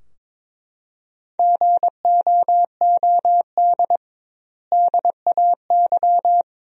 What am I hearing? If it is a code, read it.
Morse code, heard as a keyed tone, about 22 wpm, 700 Hz. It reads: GOOD DAY
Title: 2x Word-Spacing - Sentences from Top 200 Words - 22wpm